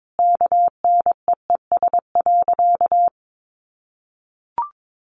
Morse code, heard as a keyed tone, about 22 wpm, 700 Hz. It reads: KNEES.
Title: A Sound Of Thunder 22wpm